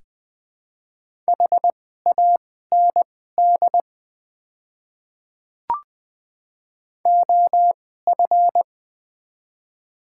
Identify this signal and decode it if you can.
Morse code, heard as a keyed tone, about 21 wpm, spaced as for 15 wpm, 700 Hz. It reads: HAND OF
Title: 2x Character Spacing - Top 300 Words - 20wpm